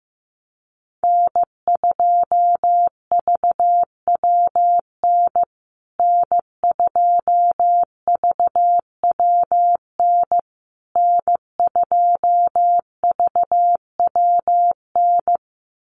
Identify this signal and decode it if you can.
Morse code, heard as a keyed tone, about 15 wpm, 700 Hz. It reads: N2VWN N2VWN N2VWN
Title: USA Call Signs Repeated 3x 15wpm